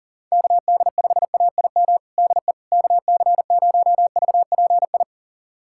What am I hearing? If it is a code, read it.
Morse code, heard as a keyed tone, about 39 wpm, 700 Hz. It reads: KD5AIM DE KC0VPI
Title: USA Callsign DE Callsign 40wpm